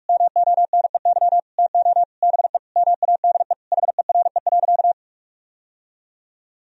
Morse code, heard as a keyed tone, about 45 wpm, 700 Hz. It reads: MONEY TO BE MADE HERE.
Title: Sentences from Top 1000 Words - Review Part 4 - 45wpm